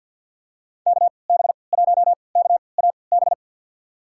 Morse code, heard as a keyed tone, about 47 wpm, spaced as for 28 wpm, 700 Hz. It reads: KB1KAB
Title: USA Call Signs - 6 Characters - 3x Character Spacing - 50wpm